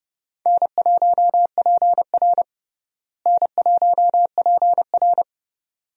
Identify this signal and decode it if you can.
Morse code, heard as a keyed tone, about 30 wpm, 700 Hz. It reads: N1PR N1PR
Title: USA Call Signs - 4 Characters - Repeated 2x - 30wpm